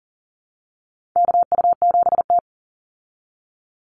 Morse code, heard as a keyed tone, about 40 wpm, 700 Hz. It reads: KU7T
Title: Call Signs Repeated 3x - 40wpm